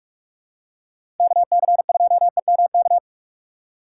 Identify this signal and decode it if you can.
Morse code, heard as a keyed tone, about 46 wpm, 700 Hz. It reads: KC2EMK
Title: USA Call Signs - 3x Character Spacing - 45wpm